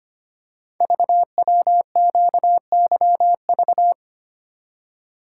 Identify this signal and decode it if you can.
Morse code, heard as a keyed tone, about 25 wpm, 700 Hz. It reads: VWQYV
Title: Code Groups - 5 Letters - 25wpm